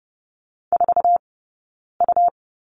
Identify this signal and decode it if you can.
Morse code, heard as a keyed tone, about 30 wpm, spaced as for 9 wpm, 690 Hz. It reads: 4U
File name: Lesson 18d - Letter m - Two Characters 30wpm